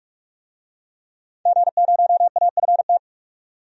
Morse code, heard as a keyed tone, about 46 wpm, 700 Hz. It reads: G0AFT
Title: DXCC-100 Call Signs - 5 Characters - 45wpm